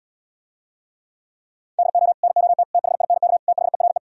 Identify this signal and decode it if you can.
Morse code, heard as a keyed tone, about 77 wpm, 700 Hz. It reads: DO NOT THINK THERE